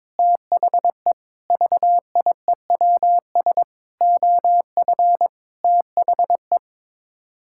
Morse code, heard as a keyed tone, about 22 wpm, 700 Hz. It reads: THE VIEWS OF THE
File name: practice / 22wpm / Sets of 8 Words - 22wpm